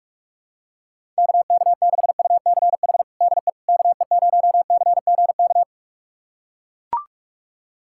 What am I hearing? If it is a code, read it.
Morse code, heard as a keyed tone, about 41 wpm, 700 Hz. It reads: KK6UCH DE KE0CGK
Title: USA Callsign DE Callsign 45wpm